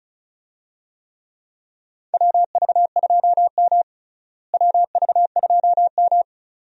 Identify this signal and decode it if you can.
Morse code, heard as a keyed tone, about 36 wpm, 700 Hz. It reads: WV2M WV2M
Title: USA Call Signs - 4 Characters - Repeated 2x - 35wpm